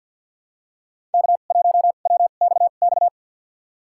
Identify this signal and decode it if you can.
Morse code, heard as a keyed tone, about 48 wpm, spaced as for 35 wpm, 690 Hz. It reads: K1WXX
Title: USA Call Signs - 5 Characters - 2x Character Spacing - 50wpm